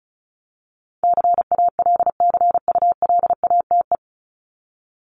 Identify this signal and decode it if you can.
Morse code, heard as a keyed tone, about 34 wpm, 700 Hz. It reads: CALCULATE
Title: Top 1751-2000 Words 35wpm